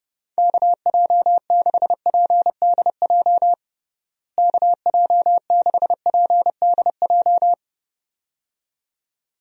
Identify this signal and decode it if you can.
Morse code, heard as a keyed tone, about 32 wpm, 700 Hz. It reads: KJ6PDJ KJ6PDJ
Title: USA Call Signs - 6 Characters - Repeated 2x - 30wpm